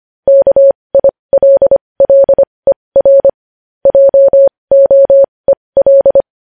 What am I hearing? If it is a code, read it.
Morse code, heard as a keyed tone, about 25 wpm, spaced as for 20 wpm, 550 Hz. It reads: KILLER JOEL
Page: news-headlines-in-morse-code-at-20-wpm